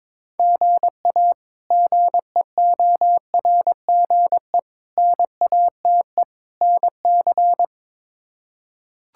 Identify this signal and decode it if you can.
Morse code, heard as a keyed tone, about 22 wpm, 700 Hz. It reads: GA GEORGE NATE NC
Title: K1USN SST QSO Practice - 22wpm